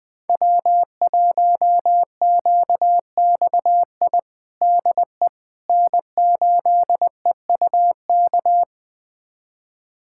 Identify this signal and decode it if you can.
Morse code, heard as a keyed tone, about 21 wpm, 700 Hz. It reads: W1QXI DE N8EUK
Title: USA Callsign DE Callsign - 5 Characters - 20wpm